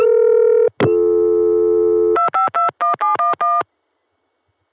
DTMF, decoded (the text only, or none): 2521711